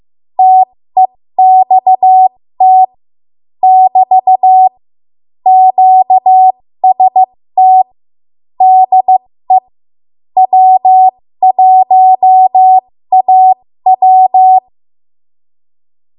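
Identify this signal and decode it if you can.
Morse code, heard as a keyed tone, about 15 wpm, spaced as for 13 wpm, 750 Hz. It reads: TEXT <BT> QST DE W1AW
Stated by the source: 13 wpm Morse code practice archive transmission.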